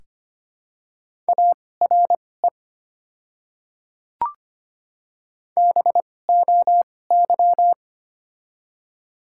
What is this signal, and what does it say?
Morse code, heard as a keyed tone, about 25 wpm, spaced as for 18 wpm, 700 Hz. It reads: ARE BOY
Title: 2x Character Spacing - Top 300 Words - 25wpm